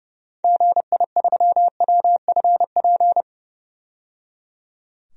Morse code, heard as a keyed tone, about 30 wpm, 700 Hz. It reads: GI3WFP
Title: DXCC-100 Call Signs - 6 Characters - 30wpm